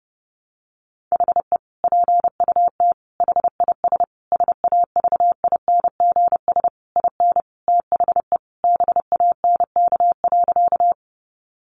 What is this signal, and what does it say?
Morse code, heard as a keyed tone, about 30 wpm, 700 Hz. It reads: HE PUT HIS SAVINGS IN THE BANK.
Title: Sentences from Top 1200 Words 30wpm